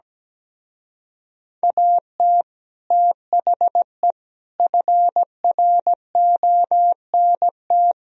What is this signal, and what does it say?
Morse code, heard as a keyed tone, about 17 wpm, 700 Hz. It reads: AT THE FRONT
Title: Speed-Racing - Sets of 5 Words 17wpm